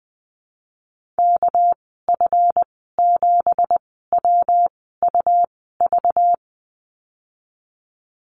Morse code, heard as a keyed tone, about 20 wpm, spaced as for 15 wpm, 700 Hz. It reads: KF7WUV
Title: 213 Lesson 34g - Number 8 - Call Signs 20wpm